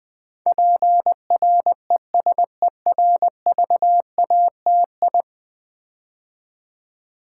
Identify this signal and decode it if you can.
Morse code, heard as a keyed tone, about 20 wpm, 700 Hz. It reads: PRESERVATI
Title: Build A Word - 5 Letters At A Time - 12-Letter Words - 20wpm